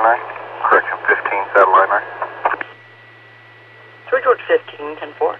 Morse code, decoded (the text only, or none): none